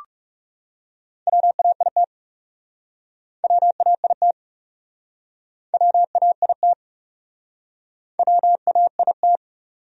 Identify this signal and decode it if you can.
Morse code, heard as a keyed tone, about 41 wpm, 700 Hz. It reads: WAIT WAIT WAIT WAIT